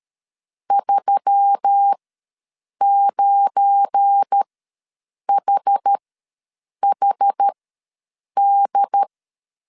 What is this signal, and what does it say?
Morse code, heard as a keyed tone, about 13 wpm, spaced as for 7 wpm, 800 Hz. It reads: 39HHD
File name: LEZIONI / 7 GF9Z / 7 GF9 ws 12wpm 3344